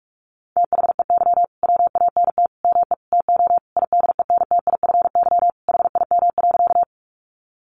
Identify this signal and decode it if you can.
Morse code, heard as a keyed tone, about 44 wpm, 700 Hz. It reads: THEY WANT ME TO IDENTIFY HIM.
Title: Sentences from Top 900 Words 45wpm